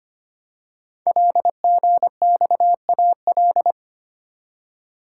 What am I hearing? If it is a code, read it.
Morse code, heard as a keyed tone, about 25 wpm, 700 Hz. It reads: LGXAL
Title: Code Groups - 5 Letters - 25wpm